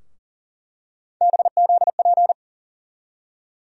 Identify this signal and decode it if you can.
Morse code, heard as a keyed tone, about 40 wpm, 700 Hz. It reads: BZP
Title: USA Call Sign Postfixes - 3 Letters 40wpm